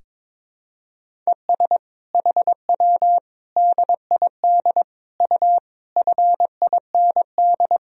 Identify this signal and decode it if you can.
Morse code, heard as a keyed tone, about 22 wpm, 700 Hz. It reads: ES HW DID U FIND
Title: Sentences from Top 200 Words and Abbreviations 22wpm